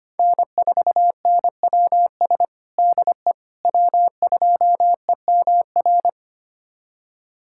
Morse code, heard as a keyed tone, about 25 wpm, 700 Hz. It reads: N4NWS DE W2EMR
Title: USA Callsign DE Callsign - 5 Characters - 25wpm